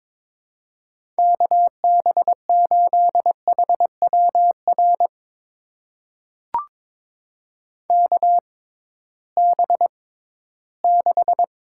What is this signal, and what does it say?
Morse code, heard as a keyed tone, about 22 wpm, spaced as for 14 wpm, 700 Hz. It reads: KB8HWR KB6
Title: USA Call Signs - 6 Characters - 6x Character Spacing - 22wpm